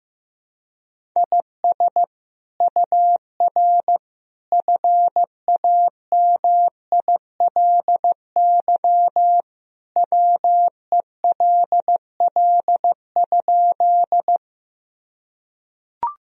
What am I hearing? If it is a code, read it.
Morse code, heard as a keyed tone, about 15 wpm, 700 Hz. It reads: IS UR FAMILY WELL?
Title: Sentences from Top 400 Words and Abbreviations 15wpm